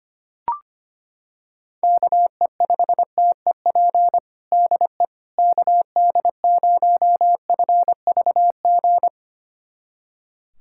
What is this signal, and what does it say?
Morse code, heard as a keyed tone, about 25 wpm, 700 Hz. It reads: KE5TEP DE KD0FVG
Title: USA Callsign DE Callsign - 6 Characters - 25wpm